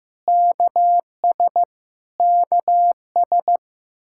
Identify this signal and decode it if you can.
Morse code, heard as a keyed tone, about 15 wpm, 700 Hz. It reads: KS KS
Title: K1USN SST QSO Practice 15wpm